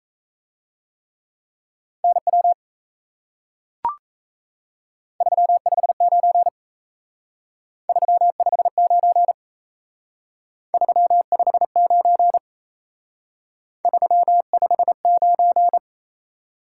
Morse code, heard as a keyed tone, about 33 wpm, 700 Hz. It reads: NW 359 359 359 359
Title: Speed-Racing Top 180 QSO Elements 28wpm